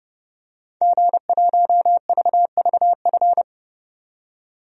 Morse code, heard as a keyed tone, about 31 wpm, 700 Hz. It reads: G1VVF
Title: DXCC-100 Call Signs - 5 Characters - 30wpm